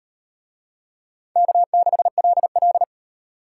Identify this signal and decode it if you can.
Morse code, heard as a keyed tone, about 35 wpm, 700 Hz. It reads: K6LL